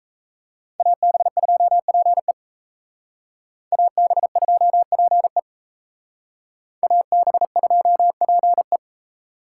Morse code, heard as a keyed tone, about 36 wpm, 700 Hz. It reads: AB2PE AB2PE AB2PE